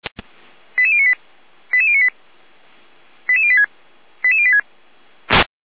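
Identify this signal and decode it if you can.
Five-tone selective call (ZVEI), heard as quick runs of five tones, tones of about 70 ms. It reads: 88098 88098 88086 88086
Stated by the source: ZVEI five-tone selective call received on an SDR.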